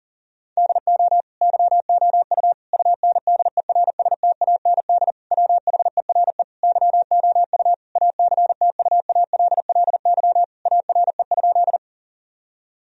Morse code, heard as a keyed tone, about 39 wpm, 700 Hz. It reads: DO YOU UNDERSTAND WHERE YOU ACTUALLY ARE?
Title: Sentences from Top 1000 Words - Review Part 4 - 40wpm